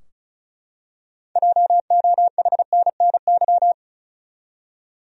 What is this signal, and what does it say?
Morse code, heard as a keyed tone, about 36 wpm, 700 Hz. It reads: JOHNNY
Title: USA Common Names - 35wpm